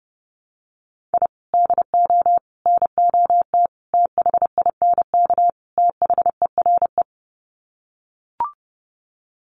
Morse code, heard as a keyed tone, about 28 wpm, 700 Hz. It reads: I DO NOT THINK THERE
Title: Sets of 5 Words 30wpm